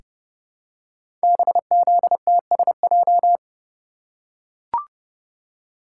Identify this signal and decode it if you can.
Morse code, heard as a keyed tone, about 30 wpm, 700 Hz. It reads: BZTSJ
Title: Code Groups - 5 Letters - 30wpm